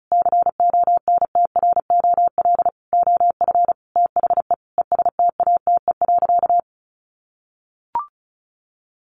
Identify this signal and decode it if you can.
Morse code, heard as a keyed tone, about 34 wpm, 700 Hz. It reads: CONTROL OF THE ESTATE.
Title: Sentences from Top 1200 Words 35wpm